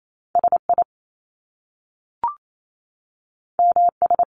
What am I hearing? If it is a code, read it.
Morse code, heard as a keyed tone, about 28 wpm, 700 Hz. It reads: SI MS
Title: Top Two Letter Combinations in Words 28wpm